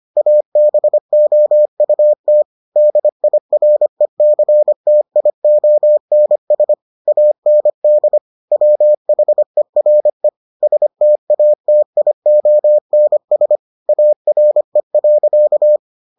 Morse code, heard as a keyed tone, about 25 wpm, 600 Hz. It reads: ABOUT DIRECTIONS AND WHERE STATIONS ARE.